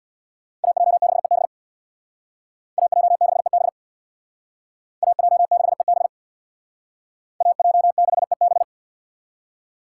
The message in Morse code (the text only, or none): AJ6EB AJ6EB AJ6EB AJ6EB